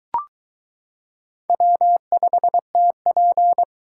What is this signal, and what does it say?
Morse code, heard as a keyed tone, about 23 wpm, 700 Hz. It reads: W5TP